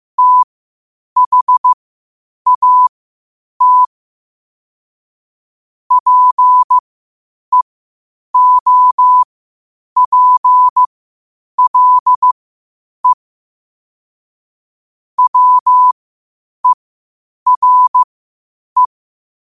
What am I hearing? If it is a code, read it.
Morse code, heard as a keyed tone, about 15 wpm, spaced as for 8 wpm, 1000 Hz. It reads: THAT PEOPLE WERE